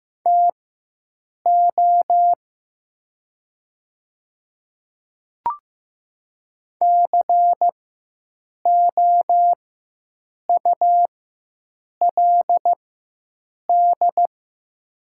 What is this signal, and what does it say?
Morse code, heard as a keyed tone, about 15 wpm, spaced as for 7 wpm, 700 Hz. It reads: TO COULD
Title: 4x Character Spacing - Top 100 Words - 15wpm